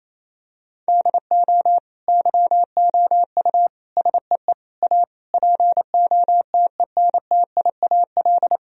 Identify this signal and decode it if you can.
Morse code, heard as a keyed tone, about 28 wpm, 700 Hz. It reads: DO YOU SEE A POTENTIAL
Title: Sentences from Top 1000 Words - Review Part 4 - 28wpm